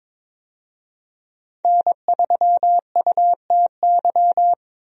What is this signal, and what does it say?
Morse code, heard as a keyed tone, about 22 wpm, 700 Hz. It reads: N3UTY
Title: USA Call Signs - 5 Characters - Repeated 2x - 22wpm